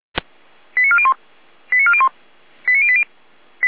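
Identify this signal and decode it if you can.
Five-tone selective call (ZVEI), heard as quick runs of five tones, tones of about 70 ms. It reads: 89401 89401 89080
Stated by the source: ZVEI five-tone selective call received on an SDR.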